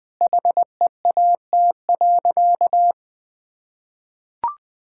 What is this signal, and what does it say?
Morse code, heard as a keyed tone, about 21 wpm, 700 Hz. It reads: HEAT.
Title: Sentences from Top 1000 Words - Review Part 6 - 20wpm